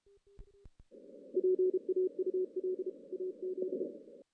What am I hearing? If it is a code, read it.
Morse code, heard as a keyed tone, about 36 wpm, 410 Hz. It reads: PAUL AZ
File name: 2016 JAN CW NAQP / 2016 Jan naqp k5ot - 0-6